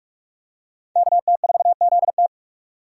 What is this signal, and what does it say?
Morse code, heard as a keyed tone, about 45 wpm, 700 Hz. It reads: KT4ZT
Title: USA Call Signs - 5 Characters - 45wpm